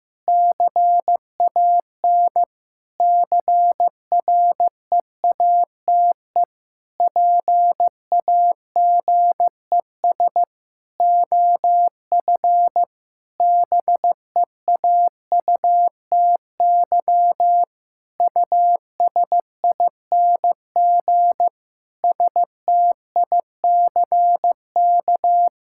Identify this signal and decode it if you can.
Morse code, heard as a keyed tone, about 15 wpm, 700 Hz. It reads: CAN CREATE PAGES OF BEAUTY USING STICK
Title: Pangram Challenge v2 - 15wpm